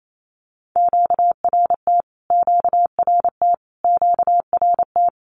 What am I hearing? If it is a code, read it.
Morse code, heard as a keyed tone, about 28 wpm, 700 Hz. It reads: QRT QRT QRT
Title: Q-Codes Repeated 3x 28wpm